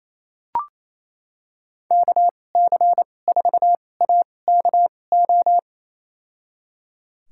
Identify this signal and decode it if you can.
Morse code, heard as a keyed tone, about 28 wpm, spaced as for 20 wpm, 700 Hz. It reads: KC4AKO
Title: USA Call Signs - 6 Characters - 2x Character Spacing - 28wpm